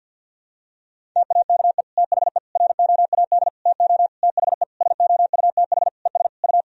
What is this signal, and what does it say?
Morse code, heard as a keyed tone, about 51 wpm, 700 Hz. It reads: TAKE THE ROAD TO THE SOUTH ES U
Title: Sentences from Top 400 Words and Abbreviations 50wpm